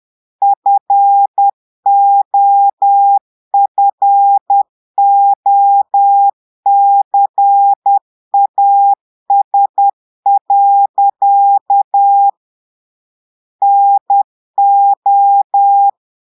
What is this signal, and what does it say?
Morse code, heard as a keyed tone, about 10 wpm, 800 Hz. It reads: FOFOCAS. NO
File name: portugais texte en clair 2 vitesse 10 mots minutes